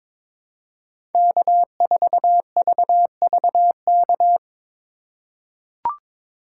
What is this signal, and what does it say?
Morse code, heard as a keyed tone, about 22 wpm, 700 Hz. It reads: K4VVK
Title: USA Call Signs - 5 Characters - 2x Character Spacing - 22wpm